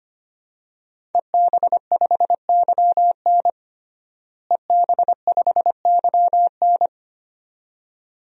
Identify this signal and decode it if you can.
Morse code, heard as a keyed tone, about 25 wpm, 700 Hz. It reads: EB5YN EB5YN
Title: DXCC-100 Call Signs - 5 Characters - Repeated 2x - 25wpm